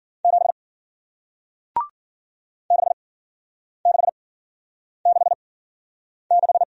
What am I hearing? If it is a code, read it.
Morse code, heard as a keyed tone, about 53 wpm, spaced as for 9 wpm, 700 Hz. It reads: 7 6666